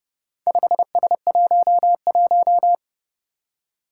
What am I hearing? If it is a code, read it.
Morse code, heard as a keyed tone, about 30 wpm, 700 Hz. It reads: 5S11
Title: Four Challenging Letter-Number Combinations 30wpm